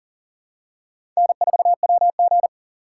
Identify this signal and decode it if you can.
Morse code, heard as a keyed tone, about 40 wpm, 700 Hz. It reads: N4WG
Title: USA Call Signs - 4 Characters - 4x Character Spacing - 40wpm